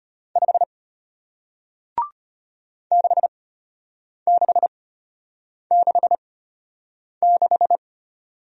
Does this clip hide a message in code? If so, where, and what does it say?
Yes: Morse code from 0.4 to 7.8 s, heard as a keyed tone, about 33 wpm, spaced as for 8 wpm, 700 Hz. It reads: 5 6666